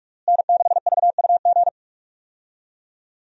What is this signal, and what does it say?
Morse code, heard as a keyed tone, about 46 wpm, 700 Hz. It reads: N6VUG
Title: USA Call Signs - 5 Characters - Repeated 2x - 45wpm